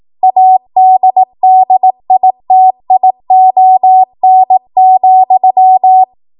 Morse code, heard as a keyed tone, about 18 wpm, 750 Hz. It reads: ADDITION,
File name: Morse / 18WPM